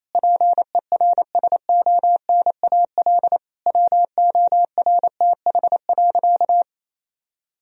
Morse code, heard as a keyed tone, about 28 wpm, 700 Hz. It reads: PERSONAL WORTH.
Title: Sentences from Top 1000 Words - Review Part 4 - 28wpm